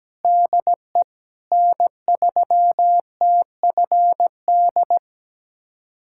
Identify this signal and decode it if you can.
Morse code, heard as a keyed tone, about 17 wpm, 700 Hz. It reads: DE N3TFD
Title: USA Callsign DE Callsign - 5 Characters - 17wpm